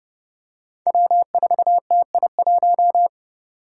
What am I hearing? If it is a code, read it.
Morse code, heard as a keyed tone, about 31 wpm, 700 Hz. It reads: W4TI1
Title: Five Challenging Letter-Number Combinations 30wpm